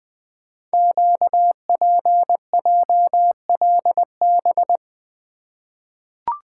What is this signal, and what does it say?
Morse code, heard as a keyed tone, about 20 wpm, 700 Hz. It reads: QPJLB
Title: Code Groups - 5 Letters - 20wpm